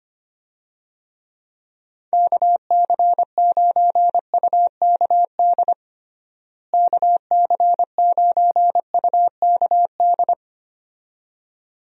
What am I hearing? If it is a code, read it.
Morse code, heard as a keyed tone, about 25 wpm, 700 Hz. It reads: KC9UKD KC9UKD
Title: USA Call Signs - 6 Characters - Repeated 2x - 25wpm